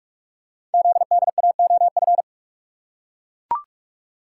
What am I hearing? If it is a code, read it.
Morse code, heard as a keyed tone, about 44 wpm, 700 Hz. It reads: ZDAOF
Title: Code Groups - 5 Letter:Numbers - 45wpm